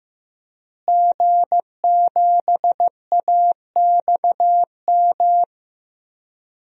Morse code, heard as a keyed tone, about 16 wpm, 700 Hz. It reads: G7AXM
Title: DXCC-100 Call Signs - 5 Characters - Repeated 2x - 15wpm